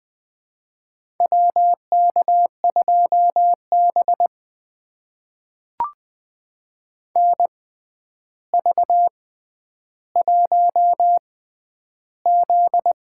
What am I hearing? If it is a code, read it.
Morse code, heard as a keyed tone, about 20 wpm, spaced as for 10 wpm, 700 Hz. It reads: WK2B NV1Z
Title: USA Call Signs - 4 Characters - 6x Character Spacing - 20wpm